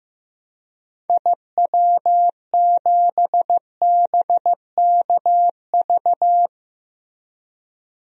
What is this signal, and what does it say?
Morse code, heard as a keyed tone, about 15 wpm, 700 Hz. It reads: IW7BKV
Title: DXCC-100 Call Signs - 6 Characters - Repeated 3x - 15wpm